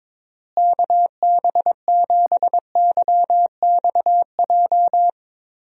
Morse code, heard as a keyed tone, about 22 wpm, 700 Hz. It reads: KB7YXJ
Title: USA Call Signs - 6 Characters - Repeated 2x - 22wpm